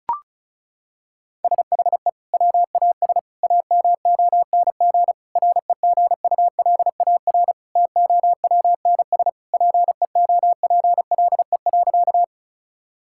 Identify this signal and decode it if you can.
Morse code, heard as a keyed tone, about 34 wpm, 700 Hz. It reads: SHE WAS AMONG REGULAR TOWNS PEOPLE.
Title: Sentences from Top 1000 Words - Review Part 6 - 35wpm